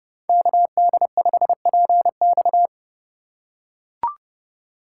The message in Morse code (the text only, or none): KD5PX